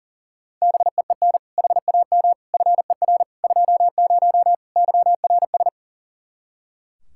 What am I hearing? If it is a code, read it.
Morse code, heard as a keyed tone, about 40 wpm, 700 Hz. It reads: BEEN HAM FER 20 YRS
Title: Practice QSO - Exchange 3 - More Info - 40wpm